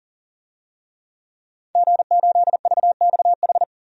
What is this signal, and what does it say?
Morse code, heard as a keyed tone, about 40 wpm, 700 Hz. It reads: G8VXH
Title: DXCC-100 Call Signs - 5 Characters - Repeated 2x - 40wpm